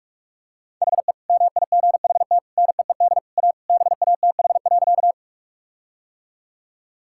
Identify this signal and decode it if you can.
Morse code, heard as a keyed tone, about 42 wpm, 700 Hz. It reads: HE MIGHT NEED A BATH.
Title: Sentences from Top 1600 Words 45wpm